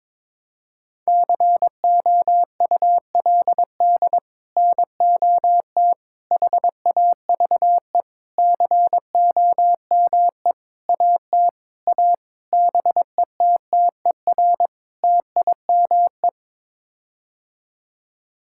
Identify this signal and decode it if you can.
Morse code, heard as a keyed tone, about 22 wpm, 700 Hz. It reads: COULD NOT HAVE COME AT A BETTER TIME
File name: practice / 22wpm / Sets of 8 Words - 22wpm